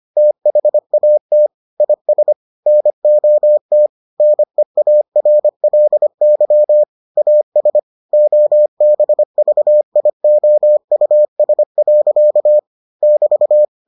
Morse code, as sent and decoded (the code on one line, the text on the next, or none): - .... .- - / .. ... / -. --- - / -. .- .-. .-.. -.-- / .- ... / --- -... ...- .. --- ..- ... .-.-.- / -...-
THAT IS NOT NEARLY AS OBVIOUS. <BT>